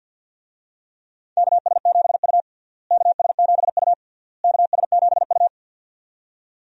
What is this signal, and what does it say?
Morse code, heard as a keyed tone, about 52 wpm, 700 Hz. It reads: KS7U KS7U KS7U
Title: USA Call Signs - 4 Characters - Repeated 3x - 50wpm